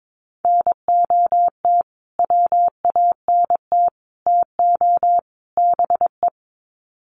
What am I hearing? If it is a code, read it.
Morse code, heard as a keyed tone, about 22 wpm, 700 Hz. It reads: NOT WANT TO BE
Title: Sets of 4 Words 22wpm